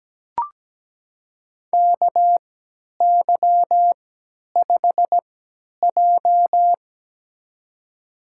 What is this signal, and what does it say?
Morse code, heard as a keyed tone, about 17 wpm, spaced as for 10 wpm, 700 Hz. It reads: KY5J
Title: USA Call Signs - 3x Character Spacing - 17wpm